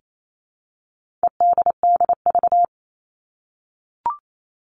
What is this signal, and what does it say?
Morse code, heard as a keyed tone, about 28 wpm, 700 Hz. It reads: EDDV
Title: All Cut Numbers - 4-Digit 28wpm